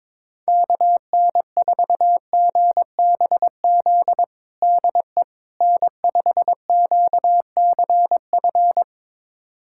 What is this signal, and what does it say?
Morse code, heard as a keyed tone, about 22 wpm, 700 Hz. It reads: KN4GBZ DE N5QCF
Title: USA Callsign DE Callsign 22wpm